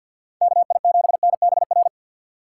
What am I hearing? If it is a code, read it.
Morse code, heard as a keyed tone, about 52 wpm, 690 Hz. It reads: KI7NBR
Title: USA Call Signs - 6 Characters - 3x Character Spacing - 50wpm